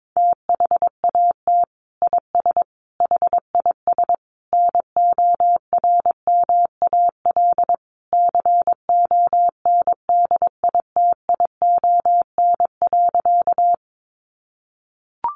Morse code, heard as a keyed tone, about 22 wpm, 700 Hz. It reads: THAT IS HIS NORMAL CONDITION.
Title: Sentences from Top 1200 Words 22wpm